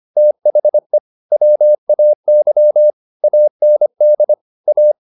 Morse code, heard as a keyed tone, about 25 wpm, 600 Hz. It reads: THE WAY AND A